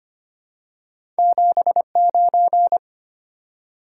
Morse code, heard as a keyed tone, about 25 wpm, 700 Hz. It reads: 79
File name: Two Letter-Number Combinations - 25wpm